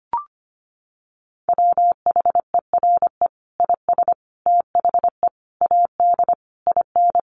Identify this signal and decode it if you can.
Morse code, heard as a keyed tone, about 25 wpm, 700 Hz. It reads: WHERE IS THE AD IN
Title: Sentences from Top 1300 Words 25wpm